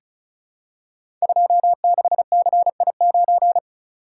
Morse code, heard as a keyed tone, about 36 wpm, 700 Hz. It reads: 26CI9
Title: Code Groups - 5 Letter:Numbers - 35wpm